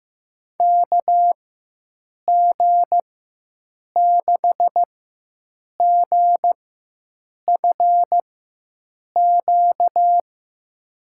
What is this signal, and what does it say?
Morse code, heard as a keyed tone, about 15 wpm, spaced as for 7 wpm, 700 Hz. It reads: KG6GFQ
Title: USA Call Signs - 6 Characters - 4x Character Spacing - 15wpm